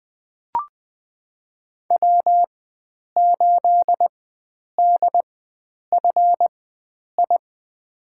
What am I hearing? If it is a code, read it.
Morse code, heard as a keyed tone, about 20 wpm, spaced as for 9 wpm, 700 Hz. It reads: W8DFI
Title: USA Call Signs - 4x Character Spacing - 20wpm